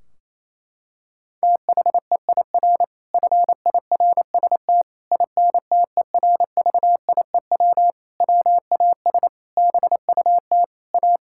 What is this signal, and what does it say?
Morse code, heard as a keyed tone, about 28 wpm, 700 Hz. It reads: THEIR FIRST INTERVIEW WAS BUT A
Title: Sentences from Top 1600 Words 28wpm